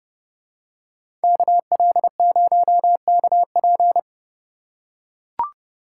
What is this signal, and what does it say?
Morse code, heard as a keyed tone, about 31 wpm, 700 Hz. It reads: KL0KP